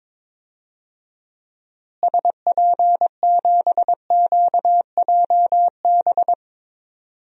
Morse code, heard as a keyed tone, about 22 wpm, 700 Hz. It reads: SP7QJB
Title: DXCC-100 Call Signs - 6 Characters - Repeated 3x - 22wpm